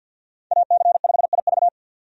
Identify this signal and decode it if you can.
Morse code, heard as a keyed tone, about 52 wpm, 700 Hz. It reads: AC5IV
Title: USA Call Signs - 5 Characters - 50wpm